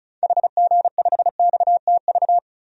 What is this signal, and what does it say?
Morse code, heard as a keyed tone, about 36 wpm, 700 Hz. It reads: HG5XTV
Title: DXCC-100 Call Signs - 6 Characters - Repeated 2x - 35wpm